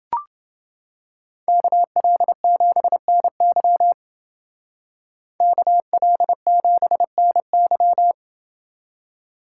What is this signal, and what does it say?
Morse code, heard as a keyed tone, about 29 wpm, 700 Hz. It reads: KL7NY KL7NY